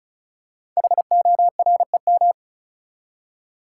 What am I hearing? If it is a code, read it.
Morse code, heard as a keyed tone, about 36 wpm, 690 Hz. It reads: HOREM
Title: Code Groups - 5 Letters - 35wpm